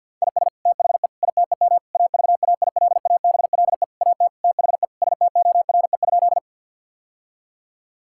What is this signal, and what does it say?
Morse code, heard as a keyed tone, about 50 wpm, 700 Hz. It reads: IS THE ITEM AVAILABLE AT THE STORE?
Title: Sentences from Top 1000 Words - Review Part 6 - 50wpm